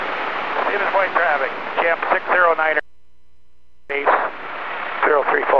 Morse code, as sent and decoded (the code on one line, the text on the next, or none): none